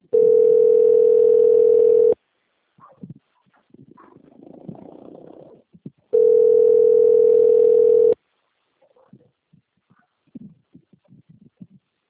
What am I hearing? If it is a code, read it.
Ringback tone (North America), 440 + 480 Hz.
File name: out- - - - - .13381